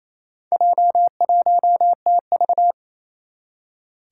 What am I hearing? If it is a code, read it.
Morse code, heard as a keyed tone, about 27 wpm, 700 Hz. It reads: J1TV
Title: Four Challenging Letter-Number Combinations 28wpm